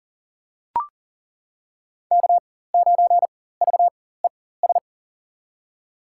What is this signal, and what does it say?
Morse code, heard as a keyed tone, about 40 wpm, spaced as for 19 wpm, 700 Hz. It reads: K9VES
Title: USA Call Signs - 5 Characters - 4x Character Spacing - 40wpm